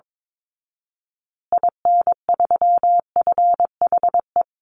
Morse code, heard as a keyed tone, about 22 wpm, 700 Hz. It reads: IN3FHE
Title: Call Signs Repeated 2x - 22wpm